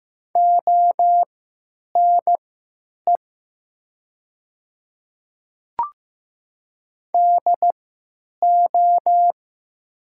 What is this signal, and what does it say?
Morse code, heard as a keyed tone, about 15 wpm, spaced as for 9 wpm, 700 Hz. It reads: ONE DO
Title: 3x Character Spacing - Top 200 Words - 15wpm